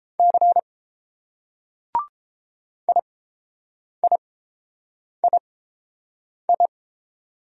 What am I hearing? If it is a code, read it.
Morse code, heard as a keyed tone, about 30 wpm, spaced as for 7 wpm, 700 Hz. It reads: C IIII